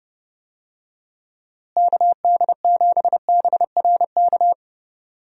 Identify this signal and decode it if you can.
Morse code, heard as a keyed tone, about 30 wpm, 700 Hz. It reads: KD7BRK
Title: USA Call Signs - 6 Characters - Repeated 3x - 30wpm